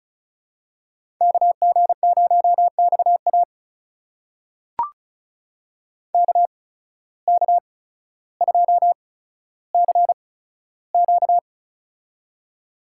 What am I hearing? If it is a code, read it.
Morse code, heard as a keyed tone, about 36 wpm, spaced as for 15 wpm, 700 Hz. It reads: KG0XA KK2CQ
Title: USA Call Signs - 5 Characters - 8x Character Spacing - 35wpm